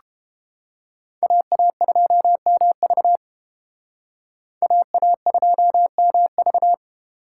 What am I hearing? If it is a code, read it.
Morse code, heard as a keyed tone, about 30 wpm, 700 Hz. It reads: AA2MV AA2MV